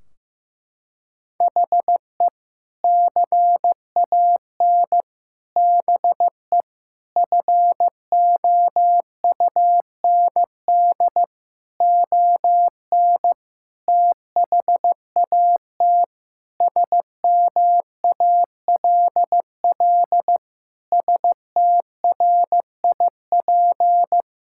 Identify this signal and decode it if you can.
Morse code, heard as a keyed tone, about 15 wpm, 700 Hz. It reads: HE CAN BE FOUND ON THAT SMALL STRIP
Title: Sentences from Top 1600 Words 15wpm